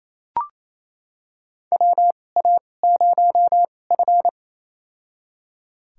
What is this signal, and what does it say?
Morse code, heard as a keyed tone, about 28 wpm, spaced as for 20 wpm, 700 Hz. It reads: WA0F
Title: USA Call Signs - 4 Characters - 2x Character Spacing - 28wpm